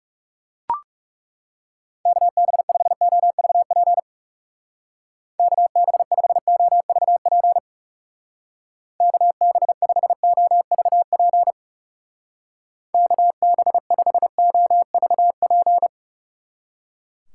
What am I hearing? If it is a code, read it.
Morse code, heard as a keyed tone, about 38 wpm, 700 Hz. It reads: KB5OVP KB5OVP KB5OVP KB5OVP